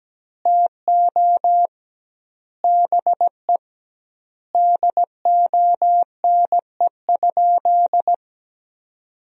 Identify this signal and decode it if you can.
Morse code, heard as a keyed tone, about 17 wpm, 700 Hz. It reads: TO BE DONE?
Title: 2x Word-Spacing - Sentences from Top 500 Words - 17wpm